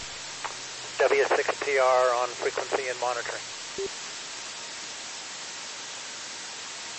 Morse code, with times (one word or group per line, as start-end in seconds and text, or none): none